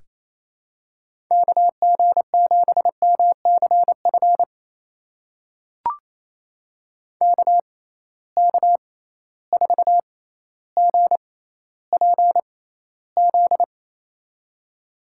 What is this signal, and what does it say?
Morse code, heard as a keyed tone, about 28 wpm, spaced as for 14 wpm, 700 Hz. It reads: KG7MCF KK4GPZ
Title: USA Call Signs - 6 Characters - 6x Character Spacing - 28wpm